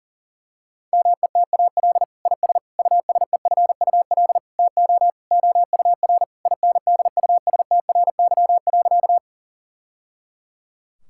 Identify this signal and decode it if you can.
Morse code, heard as a keyed tone, about 40 wpm, 690 Hz. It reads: METAL IS USEFUL TO OUR INDUSTRY.
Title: Sentences from Top 1000 Words - Review Part 6 - 40wpm